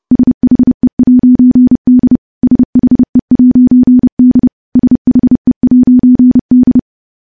Morse code, heard as a keyed tone, about 32 wpm, 250 Hz. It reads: SHE'D SHE'D SHE'D
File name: morse / morse-training-30WPM